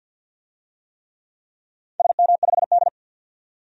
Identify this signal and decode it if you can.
Morse code, heard as a keyed tone, about 53 wpm, 700 Hz. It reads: SM5D
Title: DXCC-100 Call Signs - 4 Characters - 50wpm